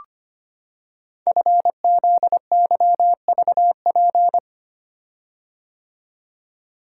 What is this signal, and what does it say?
Morse code, heard as a keyed tone, about 25 wpm, 700 Hz. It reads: FZYVP
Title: Code Groups - 5 Letter:Numbers - 25wpm